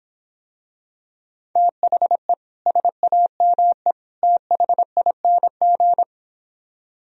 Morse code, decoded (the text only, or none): THE SAME THING